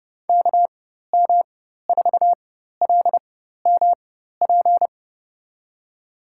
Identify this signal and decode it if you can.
Morse code, heard as a keyed tone, about 31 wpm, spaced as for 14 wpm, 700 Hz. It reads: KM4LMP
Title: USA Call Signs - 6 Characters - 4x Character Spacing - 30wpm